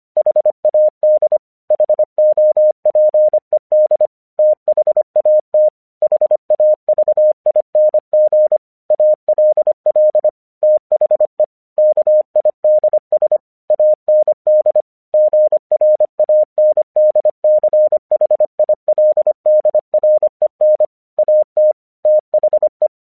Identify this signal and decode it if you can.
Morse code, heard as a keyed tone, about 25 wpm, 610 Hz. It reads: HAD HOPED THAT HAVING ALL THE KIDS AND GRANDCHILDREN AT THE
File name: 1 English 25wpm / AQC15